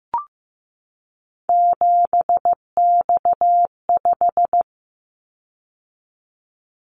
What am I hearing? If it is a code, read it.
Morse code, heard as a keyed tone, about 15 wpm, 700 Hz. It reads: 7X5
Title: Callsign Prefixes for Top 100 DXCC - 15wpm